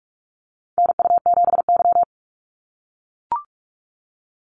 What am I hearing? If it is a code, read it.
Morse code, heard as a keyed tone, about 44 wpm, 700 Hz. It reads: NU7Y
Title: NA SOTA Activator Callsigns 45wpm